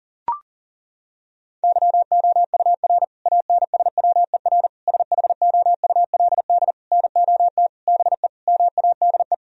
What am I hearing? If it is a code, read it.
Morse code, heard as a keyed tone, about 40 wpm, 700 Hz. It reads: YOUR ANSWER SHOULD NOT BE MADE
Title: Sentences from Top 1000 Words - Review Part 3 - 40wpm